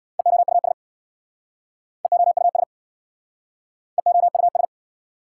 Morse code, heard as a keyed tone, about 61 wpm, 690 Hz. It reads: EOUS EOUS EOUS
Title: Speed-Racing Prefixes and Suffixes 50wpm